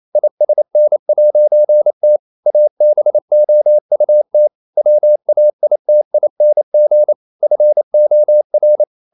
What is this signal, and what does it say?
Morse code, heard as a keyed tone, about 28 wpm, 600 Hz. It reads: ISN'T ABOUT WAITING FOR
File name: CW / Practice / EnglishQuotations0128